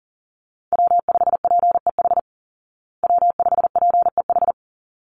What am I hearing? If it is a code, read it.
Morse code, heard as a keyed tone, about 40 wpm, 700 Hz. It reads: W5PEH W5PEH
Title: Call Signs Repeated 3x - 40wpm